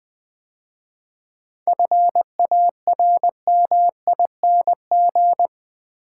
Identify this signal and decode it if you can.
Morse code, heard as a keyed tone, about 20 wpm, 700 Hz. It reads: FARMING
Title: 7 Letter Words - First 3 Letters Given - 20wpm